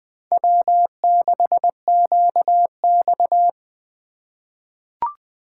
Morse code, heard as a keyed tone, about 20 wpm, 700 Hz. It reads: W6QX
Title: USA Call Signs - 4 Characters - 4x Character Spacing - 20wpm